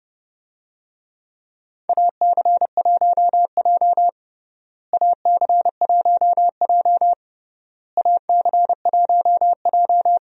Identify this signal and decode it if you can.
Morse code, heard as a keyed tone, about 32 wpm, 700 Hz. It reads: AC1J AC1J AC1J
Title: USA Call Signs - 4 Characters - Repeated 3x - 30wpm